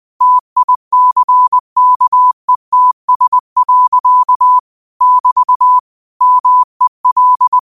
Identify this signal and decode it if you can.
Morse code, heard as a keyed tone, about 20 wpm, 1000 Hz. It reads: TICKETS. <BT> MEL
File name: qotdfiles / 20wpm